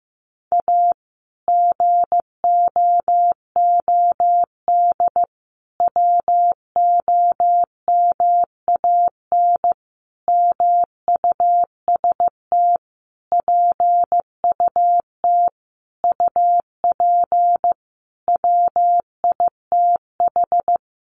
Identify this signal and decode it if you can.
Morse code, heard as a keyed tone, about 15 wpm, 700 Hz. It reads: A GOOD WOMAN MUST PUT UP WITH
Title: Sentences from Top 1100 Words 15wpm